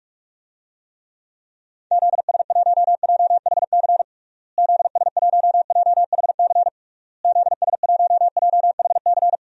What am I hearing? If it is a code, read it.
Morse code, heard as a keyed tone, about 41 wpm, 700 Hz. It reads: ZS1JHC ZS1JHC ZS1JHC
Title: DXCC-100 Call Signs - 6 Characters - Repeated 3x - 45wpm